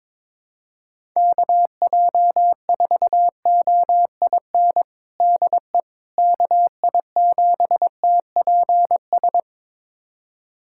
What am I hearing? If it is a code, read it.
Morse code, heard as a keyed tone, about 22 wpm, 700 Hz. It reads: KJ4OIN DE KI7TPS
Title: USA Callsign DE Callsign 22wpm